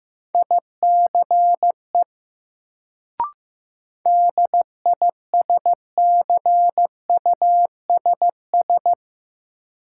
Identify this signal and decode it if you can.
Morse code, heard as a keyed tone, about 15 wpm, 700 Hz. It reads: ICE DISCUSS
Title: Sentences from Top 1000 Words - New Word Drill - 15wpm